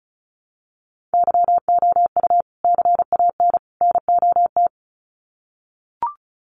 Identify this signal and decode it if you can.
Morse code, heard as a keyed tone, about 34 wpm, 700 Hz. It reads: YOU CAN NOT
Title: Sets of 3 Words 35wpm